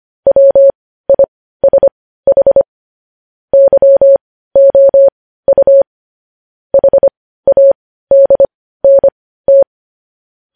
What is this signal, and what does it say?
Morse code, heard as a keyed tone, about 25 wpm, spaced as for 15 wpm, 550 Hz. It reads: WISH YOU HADNT
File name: podcast / News headlines-1x-25x15